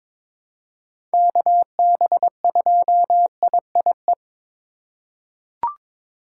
Morse code, heard as a keyed tone, about 22 wpm, 700 Hz. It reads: KB2IIE
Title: USA Call Signs - 6 Characters - 3x Character Spacing - 22wpm